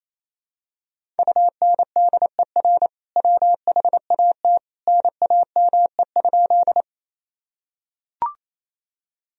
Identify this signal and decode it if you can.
Morse code, heard as a keyed tone, about 28 wpm, 700 Hz. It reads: UNDER WHAT NAME?
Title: Sentences from Top 1000 Words - Review Part 1 - 28wpm